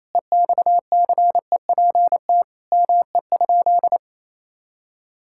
Morse code, heard as a keyed tone, about 28 wpm, 700 Hz. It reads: EXCEPT ME?
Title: Sentences from Top 1600 Words 28wpm